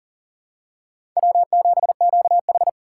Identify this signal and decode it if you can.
Morse code, heard as a keyed tone, about 38 wpm, 700 Hz. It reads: W7QH
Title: USA Call Signs - 4 Characters - 4x Character Spacing - 40wpm